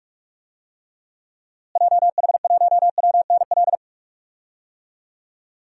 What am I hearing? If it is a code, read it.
Morse code, heard as a keyed tone, about 41 wpm, 700 Hz. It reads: JH1WNL
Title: DXCC-100 Call Signs - 6 Characters - Repeated 3x - 45wpm